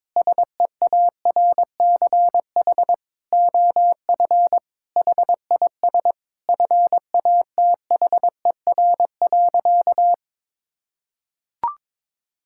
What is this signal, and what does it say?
Morse code, heard as a keyed tone, about 22 wpm, 700 Hz. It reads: SEARCH OF HIS FATHER.
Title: Sentences from Top 1000 Words - Review Part 5 - 22wpm